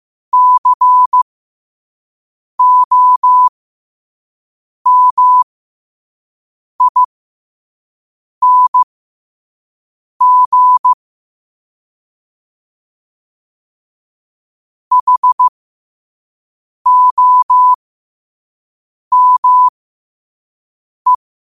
Morse code, heard as a keyed tone, about 15 wpm, spaced as for 5 wpm, 1000 Hz. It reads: COMING HOME
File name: qotdfiles / 5wpm